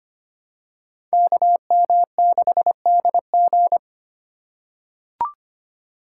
Morse code, heard as a keyed tone, about 25 wpm, 700 Hz. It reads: KM6DG
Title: USA Call Signs - 5 Characters - 3x Character Spacing - 25wpm